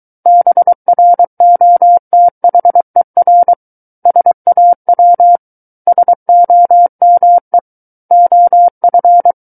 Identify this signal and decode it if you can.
Morse code, heard as a keyed tone, about 23 wpm, 700 Hz. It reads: BROTHER SAW SOME OF
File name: Book One / 14-Chapter 14 - 23 WPM